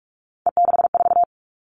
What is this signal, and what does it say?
Morse code, heard as a keyed tone, about 44 wpm, 700 Hz. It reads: E64
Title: All Cut Numbers - 3-Digit 45wpm